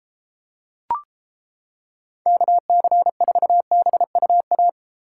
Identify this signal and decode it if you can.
Morse code, heard as a keyed tone, about 29 wpm, 700 Hz. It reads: KC4BUA